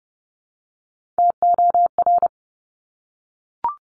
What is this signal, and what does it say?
Morse code, heard as a keyed tone, about 32 wpm, 700 Hz. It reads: TOR
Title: Top Three Letter Combinations in Words 30wpm